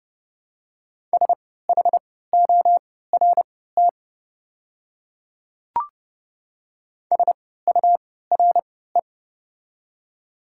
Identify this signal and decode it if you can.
Morse code, heard as a keyed tone, about 32 wpm, spaced as for 17 wpm, 700 Hz. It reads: SHORT SURE
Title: 3x Character Spacing - Top 500 Words - 30wpm